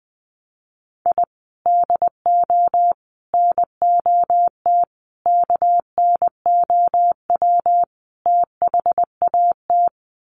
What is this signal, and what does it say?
Morse code, heard as a keyed tone, about 20 wpm, 700 Hz. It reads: I DO NOT KNOW THAT
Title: Sets of 5 Words 20wpm